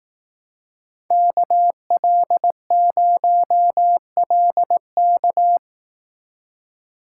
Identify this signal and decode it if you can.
Morse code, heard as a keyed tone, about 18 wpm, 700 Hz. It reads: KL0LK
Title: Speed Racing - USA 2x2 Call Signs 15wpm